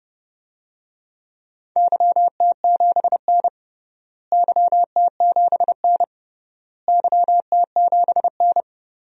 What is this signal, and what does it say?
Morse code, heard as a keyed tone, about 29 wpm, 700 Hz. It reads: YT7N YT7N YT7N
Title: DXCC-100 Call Signs - 4 Characters - Repeated 3x - 30wpm